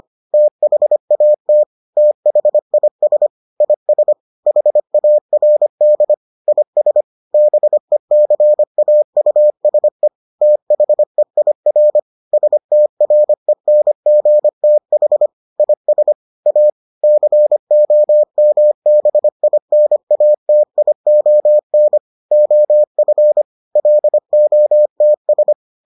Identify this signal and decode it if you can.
Morse code, heard as a keyed tone, about 25 wpm, 600 Hz. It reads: THAT THIS IS HARD IS BECAUSE THEIR STRENGTH IS A COMBINATION OF LOTS